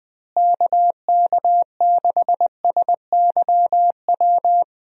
Morse code, heard as a keyed tone, about 20 wpm, 700 Hz. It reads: KK6SYW
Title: USA Call Signs - 6 Characters - Repeated 3x - 20wpm